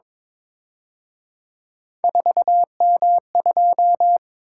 Morse code, heard as a keyed tone, about 28 wpm, 700 Hz. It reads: 4M2